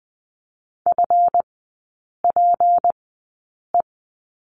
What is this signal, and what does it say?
Morse code, heard as a keyed tone, about 20 wpm, spaced as for 8 wpm, 700 Hz. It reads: FPE
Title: 198 Lesson 32e - Letter k - Three Characters 20wpm